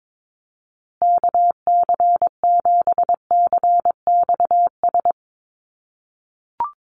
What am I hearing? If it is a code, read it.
Morse code, heard as a keyed tone, about 22 wpm, 700 Hz. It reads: KC7CXS
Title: USA Call Signs Repeated 3x 22wpm